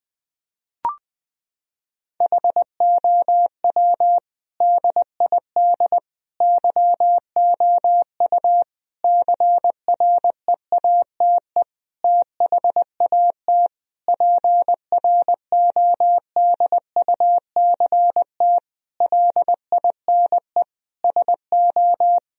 Morse code, heard as a keyed tone, about 20 wpm, 700 Hz. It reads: HOW DID YOU CREATE THAT PRODUCT LINE SO
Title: Sentences from Top 1000 Words - Review Part 3 - 20wpm